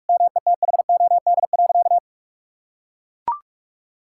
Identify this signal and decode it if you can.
Morse code, heard as a keyed tone, about 41 wpm, 700 Hz. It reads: METHOD.
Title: Sentences from Top 1000 Words - Review Part 6 - 45wpm